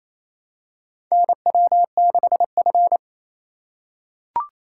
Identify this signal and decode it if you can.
Morse code, heard as a keyed tone, about 28 wpm, 700 Hz. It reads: NW6F
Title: USA Call Signs - 4 Characters - 3x Character Spacing - 28wpm